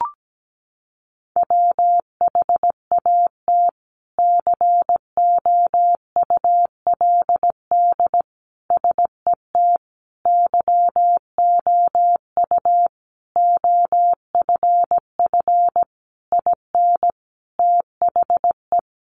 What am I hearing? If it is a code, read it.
Morse code, heard as a keyed tone, about 17 wpm, 700 Hz. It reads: WHAT COULD SET YOU OFF IN THE
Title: Sentences from Top 200 Words 17wpm